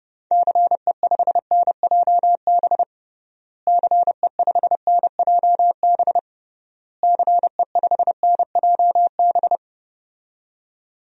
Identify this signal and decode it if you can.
Morse code, heard as a keyed tone, about 30 wpm, 700 Hz. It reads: CE5NJB CE5NJB CE5NJB
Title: DXCC-100 Call Signs - 6 Characters - Repeated 3x - 30wpm